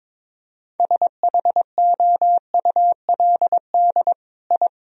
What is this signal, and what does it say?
Morse code, heard as a keyed tone, about 22 wpm, 700 Hz. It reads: SHOULD I
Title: Sentences from Top 1000 Words - Review Part 2 - 22wpm